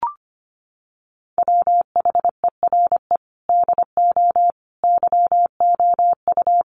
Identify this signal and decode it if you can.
Morse code, heard as a keyed tone, about 25 wpm, 700 Hz. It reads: WHERE DO YOU
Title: Sentences from Top 200 Words 25wpm